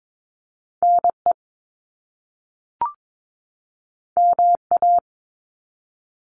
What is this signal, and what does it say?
Morse code, heard as a keyed tone, about 22 wpm, 700 Hz. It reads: NE MA
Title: US States 22wpm